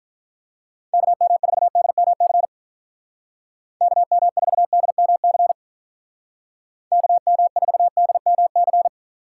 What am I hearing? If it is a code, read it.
Morse code, heard as a keyed tone, about 47 wpm, 700 Hz. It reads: KM4DMC KM4DMC KM4DMC